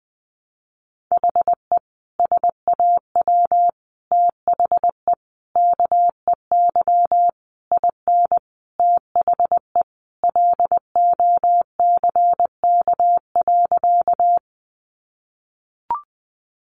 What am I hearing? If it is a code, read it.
Morse code, heard as a keyed tone, about 20 wpm, 700 Hz. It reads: HE SAW THE KEY IN THE LOCK.
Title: Sentences from Top 1300 Words 20wpm